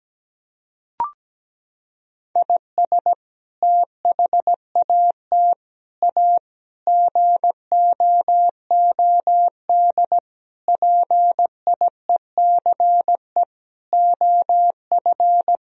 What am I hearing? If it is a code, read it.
Morse code, heard as a keyed tone, about 17 wpm, 700 Hz. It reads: IS THAT A GOOD PIECE OF
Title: Sentences from Top 1000 Words - Review Part 3 - 17wpm